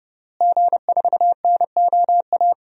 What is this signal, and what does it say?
Morse code, heard as a keyed tone, about 30 wpm, 700 Hz. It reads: G4NOA
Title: DXCC-100 Call Signs - 5 Characters - Repeated 3x - 30wpm